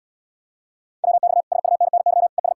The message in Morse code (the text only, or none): WX SUNNY ES